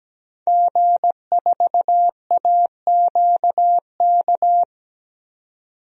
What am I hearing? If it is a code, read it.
Morse code, heard as a keyed tone, about 17 wpm, 700 Hz. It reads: G4AQK
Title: DXCC-100 Call Signs - 5 Characters - 17wpm